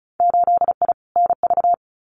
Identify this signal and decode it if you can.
Morse code, heard as a keyed tone, about 34 wpm, 700 Hz. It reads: 8I NV
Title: Winter Field Day Exchanges Encore 35wpm